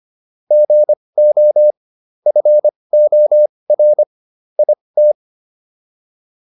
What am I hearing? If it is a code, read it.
Morse code, heard as a keyed tone, about 25 wpm, spaced as for 20 wpm, 600 Hz. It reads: GO FOR IT